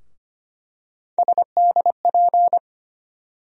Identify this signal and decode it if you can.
Morse code, heard as a keyed tone, about 25 wpm, 700 Hz. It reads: SDP
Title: USA Call Sign Postfixes - 3 Letters 25wpm